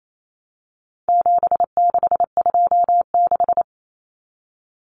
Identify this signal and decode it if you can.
Morse code, heard as a keyed tone, about 28 wpm, 700 Hz. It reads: 7626
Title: Numbers - 4-Digit 28wpm